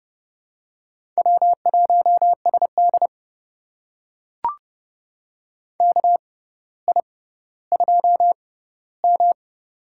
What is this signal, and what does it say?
Morse code, heard as a keyed tone, about 30 wpm, spaced as for 15 wpm, 700 Hz. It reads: W1SD KI2M
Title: USA Call Signs - 6x Character Spacing - 30wpm